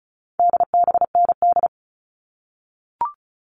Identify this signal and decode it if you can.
Morse code, heard as a keyed tone, about 34 wpm, 700 Hz. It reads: DBND
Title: All Cut Numbers - 4-Digit 35wpm